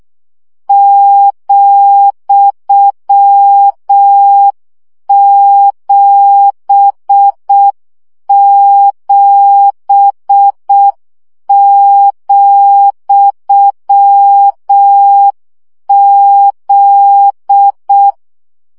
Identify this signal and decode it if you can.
Morse code, heard as a keyed tone, about 6 wpm, 800 Hz. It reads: ,77,Z
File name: CW gyakorlas z 7 vesszo 5 baud